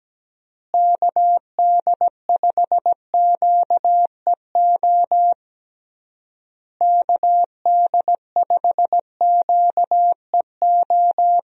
Morse code, heard as a keyed tone, about 17 wpm, 700 Hz. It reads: KD5QEO KD5QEO
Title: USA Call Signs - 6 Characters - Repeated 3x - 17wpm